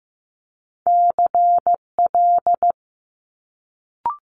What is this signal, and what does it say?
Morse code, heard as a keyed tone, about 15 wpm, 700 Hz. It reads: CL
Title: Multi-letter Phonemes Quick Fire 15wpm